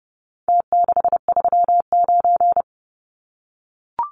Morse code, heard as a keyed tone, about 30 wpm, 700 Hz. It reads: T639
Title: Numbers - Contest Style - 4-Digit 30wpm